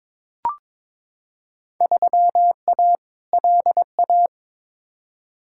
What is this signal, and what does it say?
Morse code, heard as a keyed tone, about 22 wpm, 700 Hz. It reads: 3A LA
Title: Field Day Exchanges - 22wpm